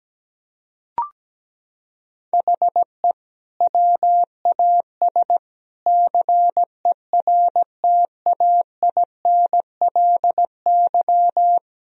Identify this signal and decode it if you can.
Morse code, heard as a keyed tone, about 17 wpm, 700 Hz. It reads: HE WAS CERTAINLY
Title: Sentences from Top 1000 Words - Review Part 1 - 17wpm